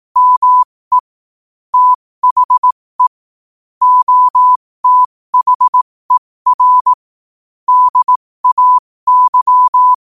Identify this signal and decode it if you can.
Morse code, heard as a keyed tone, about 18 wpm, spaced as for 15 wpm, 1000 Hz. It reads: ME THE OTHER DAY
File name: qotdfiles / 15wpm